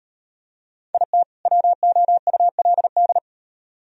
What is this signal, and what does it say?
Morse code, heard as a keyed tone, about 36 wpm, 700 Hz. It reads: IT WOULD